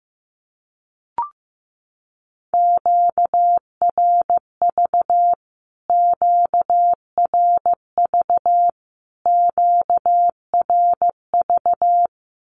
Morse code, heard as a keyed tone, about 15 wpm, 700 Hz. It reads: QRV QRV QRV
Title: Q-Codes Repeated 3x 15wpm